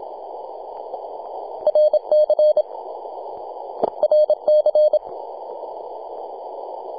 Morse code, heard as a keyed tone, about 27 wpm, 610 Hz. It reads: RC RC
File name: t88zo-28-008mhzcw